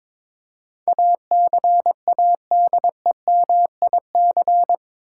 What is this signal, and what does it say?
Morse code, heard as a keyed tone, about 22 wpm, 700 Hz. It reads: ACADEMIC
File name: practice / 22wpm / Eight Letter Words - 22wpm